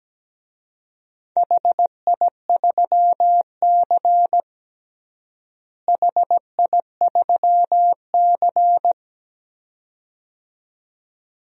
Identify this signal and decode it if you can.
Morse code, heard as a keyed tone, about 17 wpm, 700 Hz. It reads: HI3C HI3C
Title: DXCC-100 Call Signs - 4 Characters - Repeated 2x - 17wpm